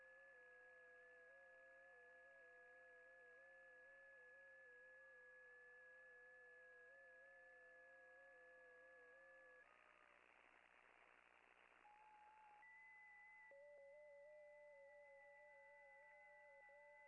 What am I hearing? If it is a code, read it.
FT8.
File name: contests / june-vhf-k5tr- -0000